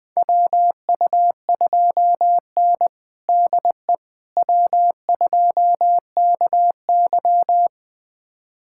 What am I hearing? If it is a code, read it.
Morse code, heard as a keyed tone, about 21 wpm, 700 Hz. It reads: WU2N DE W2KY
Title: USA Callsign DE Callsign - 4 Characters - 20wpm